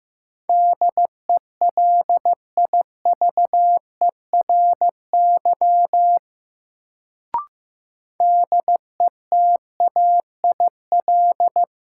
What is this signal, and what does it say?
Morse code, heard as a keyed tone, about 15 wpm, 700 Hz. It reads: DELIVERY DETAIL
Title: Sentences from Top 1500 Words - New Word Drill Alt - 15wpm